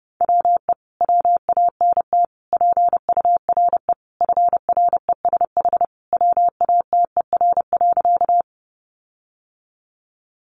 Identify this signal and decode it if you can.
Morse code, heard as a keyed tone, about 30 wpm, 700 Hz. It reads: WE WANT PURE FRESH WATER.
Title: Sentences from Top 1500 Words 30wpm